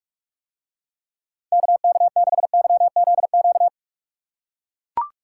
Morse code, heard as a keyed tone, about 44 wpm, 700 Hz. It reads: KK6YZQ